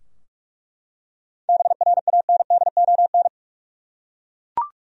Morse code, heard as a keyed tone, about 42 wpm, 700 Hz. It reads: BRANDON
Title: USA Common Names - 45wpm